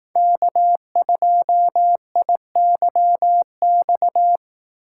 Morse code, heard as a keyed tone, about 18 wpm, 700 Hz. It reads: K2IYX
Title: Speed Racing - USA 1x3 Call Signs 15wpm